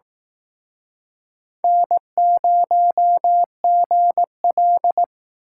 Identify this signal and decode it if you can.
Morse code, heard as a keyed tone, about 19 wpm, 700 Hz. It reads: N0GL